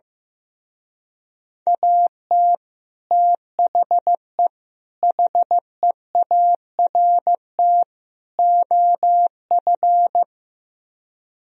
Morse code, heard as a keyed tone, about 15 wpm, 700 Hz. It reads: AT THE HEART OF
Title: Speed-Racing - Sets of 4 Words 15wpm